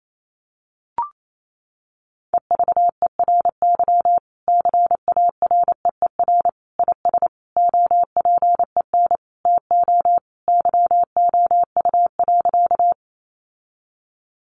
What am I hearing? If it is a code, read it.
Morse code, heard as a keyed tone, about 28 wpm, 700 Hz. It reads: EVERY CAREER IS OPEN TO YOU.
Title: Sentences from Top 1000 Words 28wpm